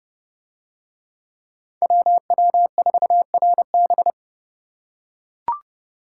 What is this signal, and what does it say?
Morse code, heard as a keyed tone, about 28 wpm, 700 Hz. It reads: WW4RB